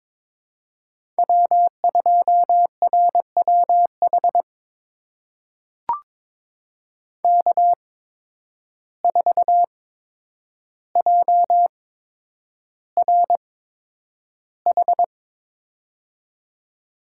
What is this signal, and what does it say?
Morse code, heard as a keyed tone, about 22 wpm, spaced as for 9 wpm, 700 Hz. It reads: W2RWH K4JRH
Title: USA Call Signs - 5 Characters - 8x Character Spacing - 22wpm